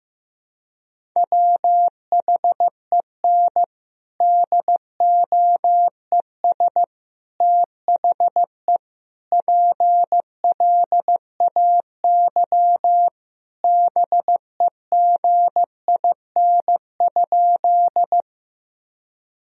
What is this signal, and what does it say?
Morse code, heard as a keyed tone, about 15 wpm, 700 Hz. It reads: WHEN DOES THE PLAY BEGIN?
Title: Sentences from Top 1000 Words - Review Part 1 - 15wpm